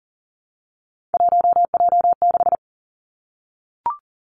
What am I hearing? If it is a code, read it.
Morse code, heard as a keyed tone, about 37 wpm, 700 Hz. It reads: 1J6
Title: Three Challenging Letter-Number Combinations 40wpm